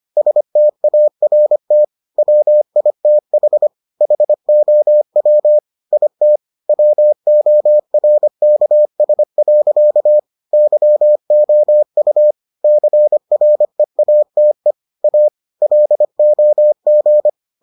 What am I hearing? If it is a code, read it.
Morse code, heard as a keyed tone, about 25 wpm, 600 Hz. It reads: START WITH HOW IT WORKS. YOU CREATE A LOG